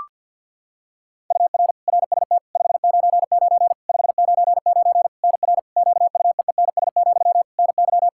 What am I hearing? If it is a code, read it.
Morse code, heard as a keyed tone, about 51 wpm, 700 Hz. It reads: UR RST 599 599 NR QUEENS, NY
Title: K1USN SST QSO Practice 50wpm